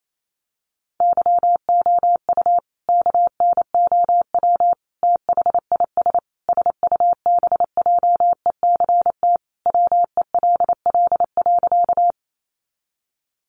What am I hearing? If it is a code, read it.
Morse code, heard as a keyed tone, about 28 wpm, 700 Hz. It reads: YOU KNOW THIS SUBJECT WELL.
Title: Sentences from Top 800 Words 28wpm